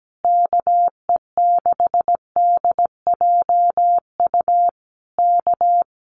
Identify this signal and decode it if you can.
Morse code, heard as a keyed tone, about 17 wpm, 700 Hz. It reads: KE6DJU K
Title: Common QSO Phrases Encore 17wpm